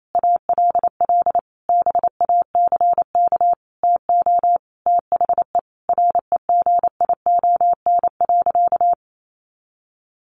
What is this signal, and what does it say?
Morse code, heard as a keyed tone, about 28 wpm, 700 Hz. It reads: ALL BACK TO THE REGION.
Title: Sentences from Top 1500 Words 28wpm